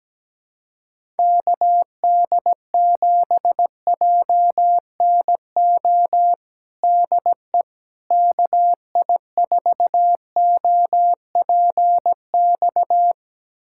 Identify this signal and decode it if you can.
Morse code, heard as a keyed tone, about 17 wpm, 700 Hz. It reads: KD7JNO DE KI4OPX
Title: USA Callsign DE Callsign 17wpm